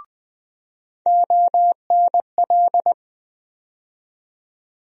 Morse code, heard as a keyed tone, about 20 wpm, 700 Hz. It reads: ONL
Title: USA Call Sign Postfixes - 3 Letters 20wpm